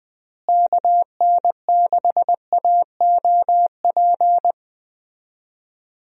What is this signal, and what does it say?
Morse code, heard as a keyed tone, about 20 wpm, 700 Hz. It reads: KN6AOP
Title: USA Call Signs - 6 Characters - Repeated 2x - 20wpm